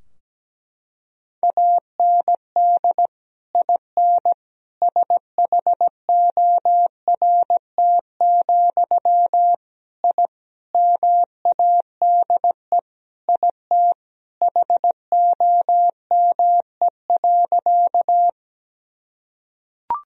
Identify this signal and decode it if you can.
Morse code, heard as a keyed tone, about 17 wpm, 700 Hz. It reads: AND IN SHORT, I MADE IT HOME.
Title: Sentences from Top 1000 Words - Review Part 2 - 17wpm